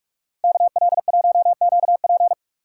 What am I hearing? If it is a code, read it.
Morse code, heard as a keyed tone, about 45 wpm, 700 Hz. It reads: KL1QP